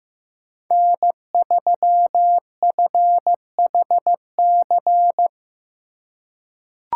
Morse code, heard as a keyed tone, about 15 wpm, 700 Hz. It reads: N3FHC
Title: USA Call Signs - 5 Characters - 3x Character Spacing - 15wpm